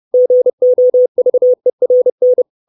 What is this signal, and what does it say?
Morse code, heard as a keyed tone, about 30 wpm, 500 Hz. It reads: GOVERN
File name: 632 Words 30WPM